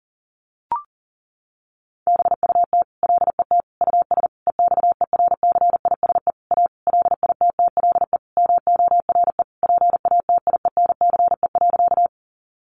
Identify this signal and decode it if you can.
Morse code, heard as a keyed tone, about 38 wpm, 700 Hz. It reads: BUT LET US EXERCISE A LITTLE MORE PATIENCE.
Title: Sentences from Top 1300 Words 40wpm